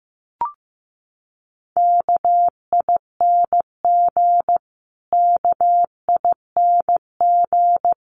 Morse code, heard as a keyed tone, about 15 wpm, 700 Hz. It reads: KING KING
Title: Top 500 Words Repeated 15wpm